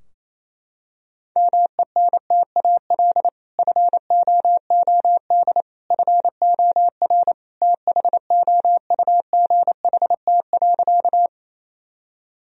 Morse code, heard as a keyed tone, about 28 wpm, 700 Hz. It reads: MENTAL FOOD FOR THOUGHT.
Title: Sentences from Top 1000 Words - Review Part 4 - 28wpm